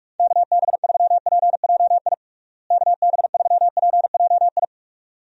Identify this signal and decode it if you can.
Morse code, heard as a keyed tone, about 46 wpm, 700 Hz. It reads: KB3PJI KB3PJI
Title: USA Call Signs - 6 Characters - Repeated 2x - 45wpm